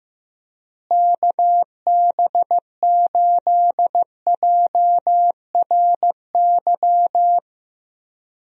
Morse code, heard as a keyed tone, about 15 wpm, 700 Hz. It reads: KB8JRY
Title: USA Call Signs - 6 Characters - Repeated 2x - 15wpm